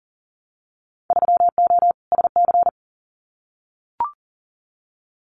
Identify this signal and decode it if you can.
Morse code, heard as a keyed tone, about 38 wpm, 700 Hz. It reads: 3O SC
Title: Winter Field Day Exchanges Encore 40wpm